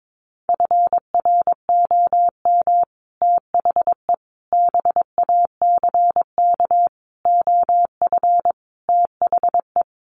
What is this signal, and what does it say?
Morse code, heard as a keyed tone, about 22 wpm, 700 Hz. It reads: FROM THE BACK OF THE
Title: Sets of 5 Words Encore 22wpm